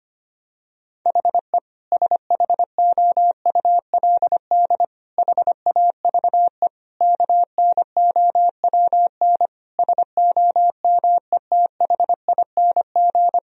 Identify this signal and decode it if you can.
Morse code, heard as a keyed tone, about 25 wpm, 700 Hz. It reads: HE SHOULD HAVE KNOWN SOMETHING
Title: Sentences from Top 1000 Words - Review Part 3 - 25wpm